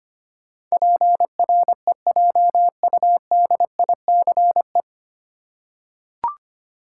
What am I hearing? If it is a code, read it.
Morse code, heard as a keyed tone, about 25 wpm, 700 Hz. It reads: PREJUDICE
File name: Nine Letter Words - 25wpm